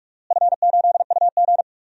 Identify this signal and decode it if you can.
Morse code, heard as a keyed tone, about 46 wpm, 700 Hz. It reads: F8UG
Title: DXCC-100 Call Signs - 4 Characters - Repeated 2x - 45wpm